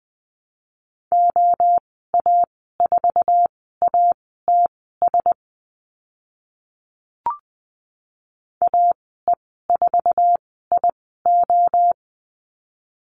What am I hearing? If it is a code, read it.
Morse code, heard as a keyed tone, about 20 wpm, spaced as for 14 wpm, 700 Hz. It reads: OA4ATS AE4IO
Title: 082 Lesson 14g - Number 2 - Call Signs 20wpm